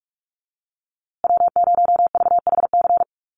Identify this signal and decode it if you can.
Morse code, heard as a keyed tone, about 44 wpm, 700 Hz. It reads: W0VHC
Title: USA Call Signs Repeated 3x 45wpm